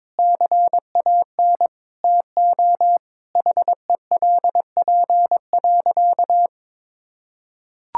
Morse code, heard as a keyed tone, about 22 wpm, 700 Hz. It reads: CAN TO HELP.
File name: practice / 22wpm / Sentences with 8-Word Phrases - 22wpm